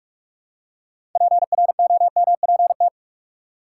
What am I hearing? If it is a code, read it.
Morse code, heard as a keyed tone, about 43 wpm, 700 Hz. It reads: PROMPT
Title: Six Letter Words - 45wpm